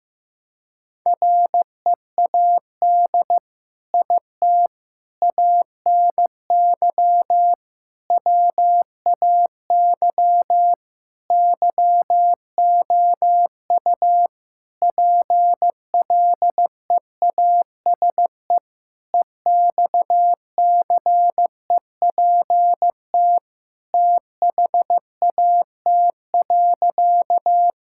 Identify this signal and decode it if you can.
Morse code, heard as a keyed tone, about 15 wpm, 700 Hz. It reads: READ IT ANY WAY YOU PLEASE EXCEPT THAT.
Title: Sentences from Top 1600 Words 15wpm